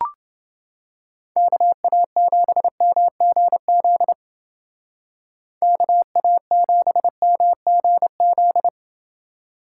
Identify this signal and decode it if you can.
Morse code, heard as a keyed tone, about 29 wpm, 700 Hz. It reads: KA7MGZ KA7MGZ